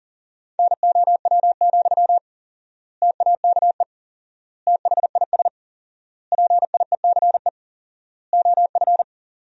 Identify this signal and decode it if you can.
Morse code, heard as a keyed tone, about 40 wpm, 700 Hz. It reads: NOW, TAKE THIS PIECE OF
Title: 4x Word-Spacing - Sentences from Top 400 Words - 40wpm